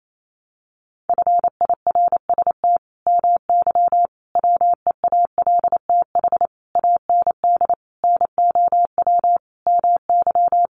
Morse code, heard as a keyed tone, about 28 wpm, 700 Hz. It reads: FIRST MY WEALTH AND NOW MY
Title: Sentences from Top 1200 Words 28wpm